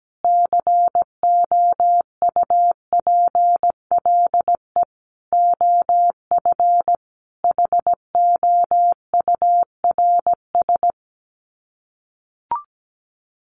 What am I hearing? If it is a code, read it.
Morse code, heard as a keyed tone, about 17 wpm, 700 Hz. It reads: COUPLE OF HOURS
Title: Sets of 5 Words Encore2 17wpm